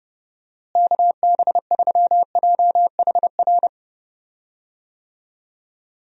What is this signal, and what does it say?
Morse code, heard as a keyed tone, about 30 wpm, 700 Hz. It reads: KB3JHR
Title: USA Call Signs - 6 Characters - Repeated 2x - 30wpm